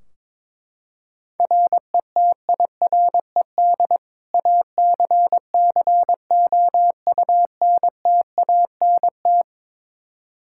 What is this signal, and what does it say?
Morse code, heard as a keyed tone, about 22 wpm, 700 Hz. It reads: RETIRED ACCOUNTANT
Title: Practice QSO - Exchange 3 - More Info - 22wpm